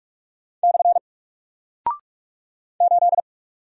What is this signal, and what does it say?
Morse code, heard as a keyed tone, about 45 wpm, 700 Hz. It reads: / 8
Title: Single Character (Letter:Number:Punctuation:Prosign) Alt 45wpm